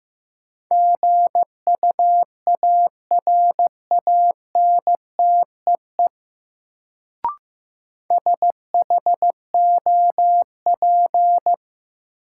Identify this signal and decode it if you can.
Morse code, heard as a keyed tone, about 15 wpm, 700 Hz. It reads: GUARANTEE SHOP
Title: Sentences from Top 1600 Words - New Word Drill Alt - 15wpm